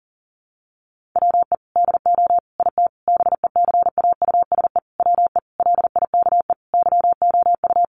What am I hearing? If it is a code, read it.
Morse code, heard as a keyed tone, about 40 wpm, 700 Hz. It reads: WE DO IT BECAUSE WE LIKE YOU
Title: Sentences from Top 1400 Words 40wpm